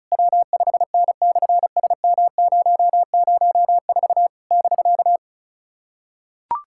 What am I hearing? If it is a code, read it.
Morse code, heard as a keyed tone, about 33 wpm, 700 Hz. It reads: W5N/SM004 <BK>
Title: SOTA QSO Practice 35wpm